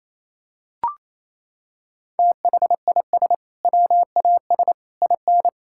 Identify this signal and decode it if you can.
Morse code, heard as a keyed tone, about 28 wpm, 700 Hz. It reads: THIS WAS IN
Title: Sentences from Top 1000 Words - Review Part 6 - 28wpm